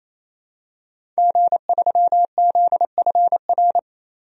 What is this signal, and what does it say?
Morse code, heard as a keyed tone, about 27 wpm, 700 Hz. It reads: G3ZFR
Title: DXCC-100 Call Signs - 5 Characters - 28wpm